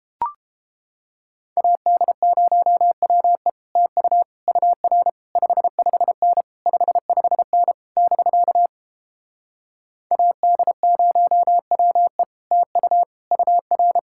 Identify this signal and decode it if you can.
Morse code, heard as a keyed tone, about 29 wpm, 700 Hz. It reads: AD0WE TU UR 55N 55N <BK> AD0WE TU UR